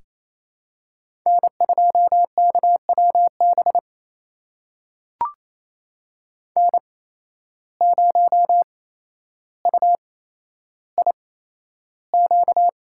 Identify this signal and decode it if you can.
Morse code, heard as a keyed tone, about 28 wpm, spaced as for 12 wpm, 700 Hz. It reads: N2KWB N0UIQ
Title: USA Call Signs - 5 Characters - 8x Character Spacing - 28wpm